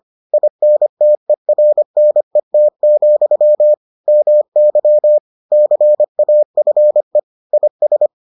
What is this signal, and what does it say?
Morse code, heard as a keyed tone, about 25 wpm, 600 Hz. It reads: INTERNET, MY CAFE IS